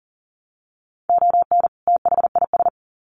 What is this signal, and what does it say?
Morse code, heard as a keyed tone, about 40 wpm, 700 Hz. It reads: ON THIS
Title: Sets of 2 Words 40wpm